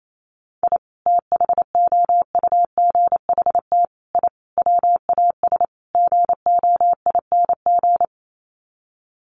Morse code, heard as a keyed tone, about 28 wpm, 700 Hz. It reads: I THOUGHT I WAS GOING
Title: Sets of 5 Words Encore2 28wpm